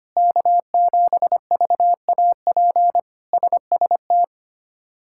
Morse code, heard as a keyed tone, about 25 wpm, 700 Hz. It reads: K7VAP SST
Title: K1USN SST QSO Practice - 25wpm